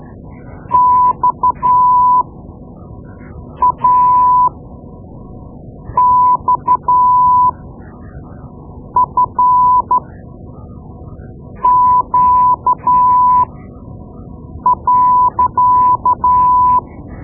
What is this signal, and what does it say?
Morse code, heard as a keyed tone, about 12 wpm, spaced as for 5 wpm, 1000 Hz. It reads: XAXFQ.
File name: Morse Lessons / Morse Lessons / Tape-2B